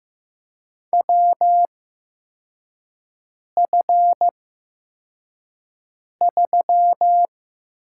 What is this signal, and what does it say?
Morse code, heard as a keyed tone, about 15 wpm, spaced as for 4 wpm, 700 Hz. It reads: WF3
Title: USA Call Signs - 4 Characters - 8x Character Spacing - 15wpm